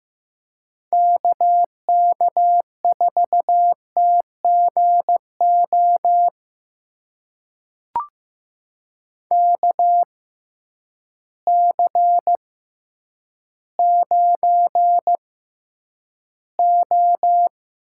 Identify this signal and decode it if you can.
Morse code, heard as a keyed tone, about 15 wpm, spaced as for 9 wpm, 700 Hz. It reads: KK4TGO KC9O
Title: USA Call Signs - 6 Characters - 6x Character Spacing - 15wpm